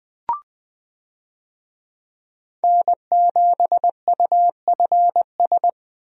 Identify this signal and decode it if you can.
Morse code, heard as a keyed tone, about 20 wpm, 700 Hz. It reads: N7UFS
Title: USA Call Signs - 5 Characters - 20wpm